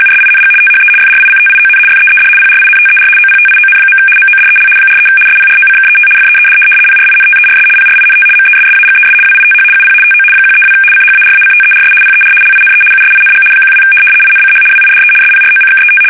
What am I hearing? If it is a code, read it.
RTTY, about 75 baud. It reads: USING THE FRICTION FEED PRINTER OPTION. MODEL 43 SPROCKET FEED PRINTERS PRINT MESSAGES WITH UP TO 132 CHARACTERS PER LINE USING 12 INCH WIDE SPROCKET
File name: fsk850-75B / teletype-corp